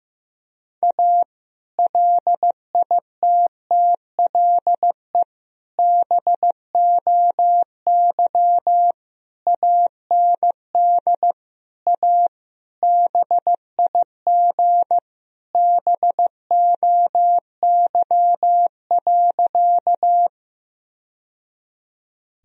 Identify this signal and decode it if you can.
Morse code, heard as a keyed tone, about 15 wpm, 700 Hz. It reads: A LITTLE BOY AND A BIG BOY.
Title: Sentences from Top 1000 Words - Review Part 1 - 15wpm